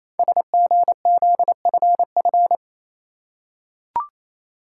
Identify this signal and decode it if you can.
Morse code, heard as a keyed tone, about 27 wpm, 700 Hz. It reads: SGZFF
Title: Code Groups - 5 Letters - 28wpm